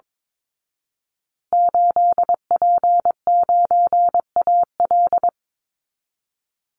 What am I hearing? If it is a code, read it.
Morse code, heard as a keyed tone, about 22 wpm, 700 Hz. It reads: 8P9AL
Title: Call Signs Repeated 2x - 22wpm